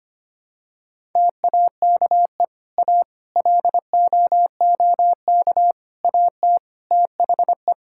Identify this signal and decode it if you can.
Morse code, heard as a keyed tone, about 25 wpm, 700 Hz. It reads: TAKE A LOOK AT THE
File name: practice / 25wpm / Sentences with 4-Word Phrases - 25wpm